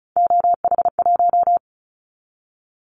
Morse code, heard as a keyed tone, about 36 wpm, 700 Hz. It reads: OH1
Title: Callsign Prefixes for Top 50 DXCC - 35wpm